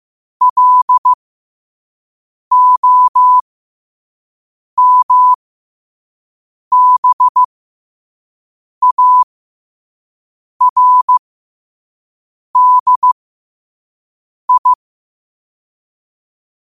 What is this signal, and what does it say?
Morse code, heard as a keyed tone, about 15 wpm, spaced as for 5 wpm, 1000 Hz. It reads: LOMBARDI
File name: qotdfiles / 5wpm